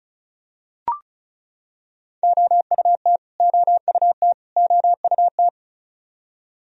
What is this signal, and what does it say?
Morse code, heard as a keyed tone, about 34 wpm, 700 Hz. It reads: OUT OUT OUT
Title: Top 100 Words Repeated 35wpm